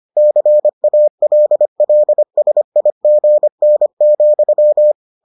Morse code, heard as a keyed tone, about 25 wpm, 600 Hz. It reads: CALLSIGN,